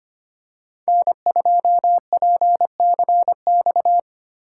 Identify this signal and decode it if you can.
Morse code, heard as a keyed tone, about 29 wpm, 700 Hz. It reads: N2PCX
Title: Speed Racing - USA 1x3 Call Signs 25wpm